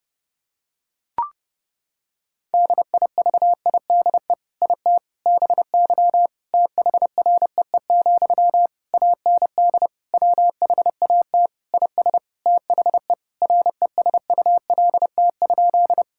Morse code, heard as a keyed tone, about 30 wpm, 700 Hz. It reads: DIVIDE IT BY THREE, AND WHAT IS THE RESULT?
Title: Sentences from Top 1600 Words 30wpm